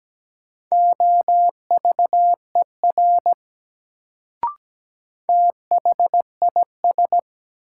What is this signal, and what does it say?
Morse code, heard as a keyed tone, about 17 wpm, 700 Hz. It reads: OVER THIS
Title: Sets of 2 Words - New Word Drill - 17wpm